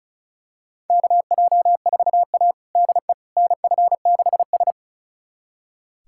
Morse code, heard as a keyed tone, about 36 wpm, 700 Hz. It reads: KJ4A DE NF6S
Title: USA Callsign DE Callsign - 4 Characters - 35wpm